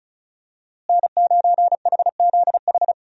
Morse code, heard as a keyed tone, about 36 wpm, 700 Hz. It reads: N9HZH
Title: USA Call Signs - 5 Characters - Repeated 3x - 35wpm